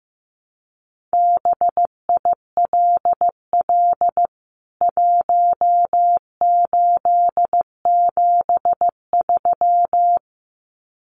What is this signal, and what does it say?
Morse code, heard as a keyed tone, about 15 wpm, 700 Hz. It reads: BILL 1873
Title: CWT Exchange 15wpm